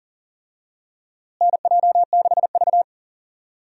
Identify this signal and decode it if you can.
Morse code, heard as a keyed tone, about 38 wpm, 700 Hz. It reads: NJ6V
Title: USA Call Signs - 4 Characters - Repeated 2x - 40wpm